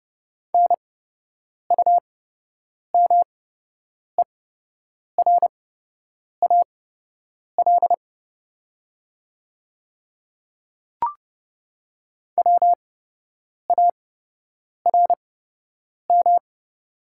Morse code, heard as a keyed tone, about 30 wpm, spaced as for 8 wpm, 700 Hz. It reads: NUMERAL WARM
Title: 8x Character Spacing - Top 500 Words - 30wpm